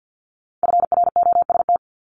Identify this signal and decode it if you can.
Morse code, heard as a keyed tone, about 50 wpm, 690 Hz. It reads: FROST
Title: Multi-letter Phonemes Repeated 3x 50wpm